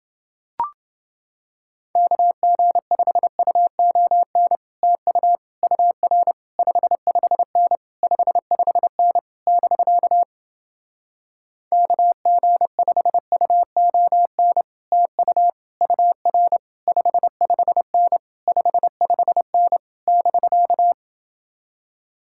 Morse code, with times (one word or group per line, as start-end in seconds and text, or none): 2.0-4.6 KG5UON
4.8-5.4 TU
5.6-6.3 UR
6.6-7.8 55N
8.0-9.2 55N
9.5-10.2 <BK>
11.7-14.6 KG5UON
14.9-15.5 TU
15.8-16.6 UR
16.9-18.2 55N
18.5-19.8 55N
20.1-20.9 <BK>